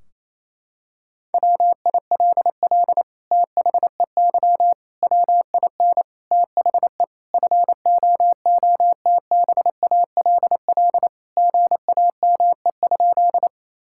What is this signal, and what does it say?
Morse code, heard as a keyed tone, about 28 wpm, 700 Hz. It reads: WILL THEY WIN THE FOOTBALL GAME?
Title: Sentences from Top 1600 Words 28wpm